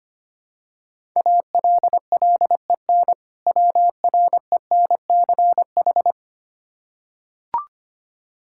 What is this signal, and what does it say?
Morse code, heard as a keyed tone, about 25 wpm, 700 Hz. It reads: ALLEN WRENCH
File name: practice / 25wpm / Difficult Words with Hints - 25wpm